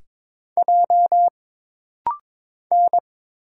Morse code, heard as a keyed tone, about 22 wpm, 700 Hz. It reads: J N
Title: Single Letter-Number - Rapid-Fire 22wpm